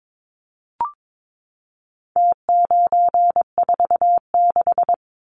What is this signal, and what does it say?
Morse code, heard as a keyed tone, about 22 wpm, 700 Hz. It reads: T946
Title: Numbers - Contest Style - 4-Digit 22wpm